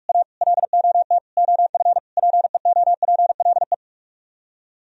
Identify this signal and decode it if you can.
Morse code, heard as a keyed tone, about 48 wpm, 700 Hz. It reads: A LOT OF PEOPLE